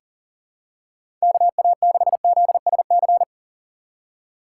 Morse code, heard as a keyed tone, about 40 wpm, 700 Hz. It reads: KA6ZSC
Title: USA Call Signs - 6 Characters - 4x Character Spacing - 40wpm